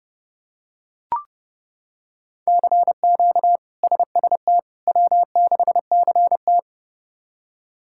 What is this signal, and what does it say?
Morse code, heard as a keyed tone, about 32 wpm, 700 Hz. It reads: CQ SST W6CT
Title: K1USN SST QSO Practice - 30wpm